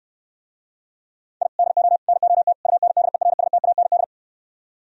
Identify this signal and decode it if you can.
Morse code, heard as a keyed tone, about 68 wpm, 700 Hz. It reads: I DO NOT UNDERSTAND